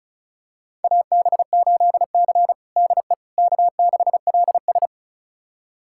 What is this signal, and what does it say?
Morse code, heard as a keyed tone, about 36 wpm, 700 Hz. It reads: AB8C DE K6LS
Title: USA Callsign DE Callsign - 4 Characters - 35wpm